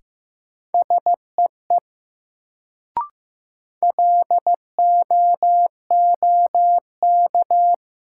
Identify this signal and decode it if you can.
Morse code, heard as a keyed tone, about 15 wpm, 700 Hz. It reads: SEE LOOK
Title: Sentences from Top 100 Words - New Word Drill - 15wpm